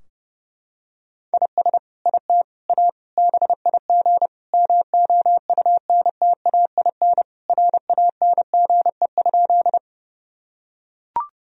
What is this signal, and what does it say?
Morse code, heard as a keyed tone, about 30 wpm, 700 Hz. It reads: IS IT A BIG MOUNTAIN RANGE?
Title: Sentences from Top 1000 Words - Review Part 4 - 30wpm